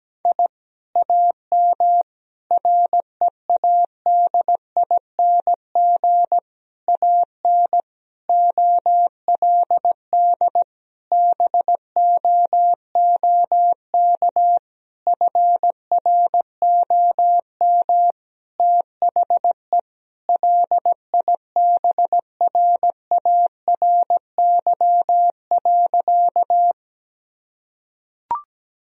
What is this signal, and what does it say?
Morse code, heard as a keyed tone, about 17 wpm, 700 Hz. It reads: I AM READING AN OLD BOOK FROM THE LIBRARY.
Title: Sentences from Top 1000 Words - Review Part 4 - 17wpm